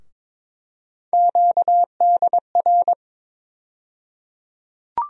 Morse code, heard as a keyed tone, about 22 wpm, 700 Hz. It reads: QDR
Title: USA Call Sign Postfixes - 3 Letters 22wpm